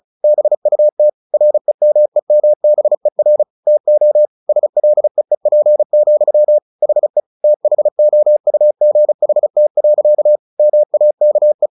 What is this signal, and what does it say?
Morse code, heard as a keyed tone, about 34 wpm, 600 Hz. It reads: BUT REMEMBER TO SLEEP, HE THOUGHT. MAKE
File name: the old man and the sea 2 35wpm